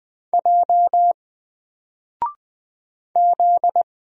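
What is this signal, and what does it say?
Morse code, heard as a keyed tone, about 20 wpm, 700 Hz. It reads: J Z
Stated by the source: Single letter and number Morse drill at 20 wpm, from N4LCD.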